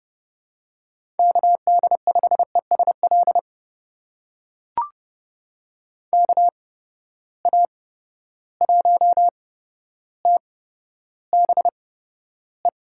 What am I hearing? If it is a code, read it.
Morse code, heard as a keyed tone, about 32 wpm, spaced as for 13 wpm, 700 Hz. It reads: KD5ESL KA1TBE
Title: USA Call Signs - 6 Characters - 8x Character Spacing - 30wpm